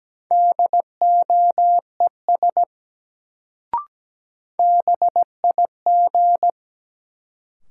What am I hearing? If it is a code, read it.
Morse code, heard as a keyed tone, about 17 wpm, 700 Hz. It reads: DOES BIG
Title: Sentences from Top 200 Words - New Word Drill - 17wpm